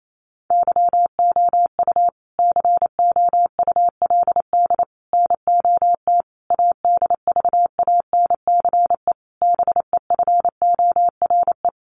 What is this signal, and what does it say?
Morse code, heard as a keyed tone, about 28 wpm, 700 Hz. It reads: YOU COULD NOT ADVANCE BEFORE
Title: Sentences from Top 1400 Words 28wpm